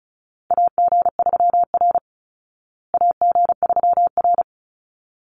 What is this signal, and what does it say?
Morse code, heard as a keyed tone, about 34 wpm, 700 Hz. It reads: AG3R AG3R
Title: Call Signs Repeated 3x - 35wpm